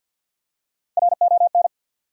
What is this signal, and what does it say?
Morse code, heard as a keyed tone, about 51 wpm, 700 Hz. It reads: RON
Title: USA Common Names - Max 5 Letters - 50wpm